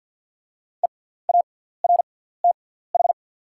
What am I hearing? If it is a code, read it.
Morse code, heard as a keyed tone, about 50 wpm, spaced as for 17 wpm, 690 Hz. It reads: EARTH
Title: 6x Character Spacing - Top 300 Words - 50wpm